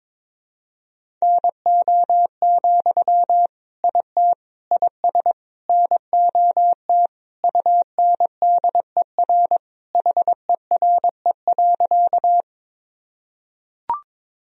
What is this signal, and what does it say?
Morse code, heard as a keyed tone, about 22 wpm, 700 Hz. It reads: NO, IT IS NOT UNDER HERE.
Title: Sentences from Top 1000 Words - Review Part 1 - 22wpm